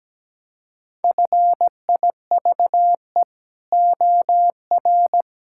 Morse code, heard as a keyed tone, about 17 wpm, 700 Hz. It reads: FIVE OR
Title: Pangram Challenge v2 - 17wpm